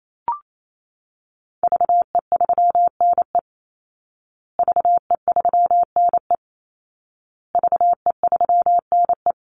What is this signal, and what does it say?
Morse code, heard as a keyed tone, about 28 wpm, 700 Hz. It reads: VE3NE VE3NE VE3NE
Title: Call Signs Repeated 2x - 28wpm